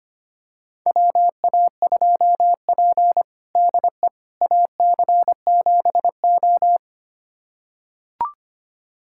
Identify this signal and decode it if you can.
Morse code, heard as a keyed tone, about 25 wpm, 700 Hz. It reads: WA2P DE AC7O
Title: USA Callsign DE Callsign - 4 Characters - 25wpm